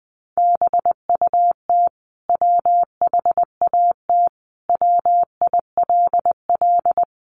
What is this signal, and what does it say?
Morse code, heard as a keyed tone, about 20 wpm, 700 Hz. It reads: BUT WHAT WILL
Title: Sentences from Top 100 Words 20wpm